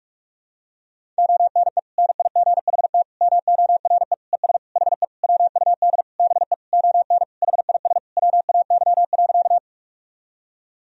Morse code, heard as a keyed tone, about 46 wpm, 700 Hz. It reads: ONE NIGHT MORE ES HE WUD BE ON HIS WAY.
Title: Sentences from Top 300 Words and Abbreviations 45wpm